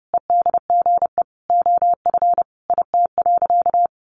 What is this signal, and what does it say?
Morse code, heard as a keyed tone, about 30 wpm, 700 Hz. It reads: EDGE OF IT.
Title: Sentences from Top 1300 Words 30wpm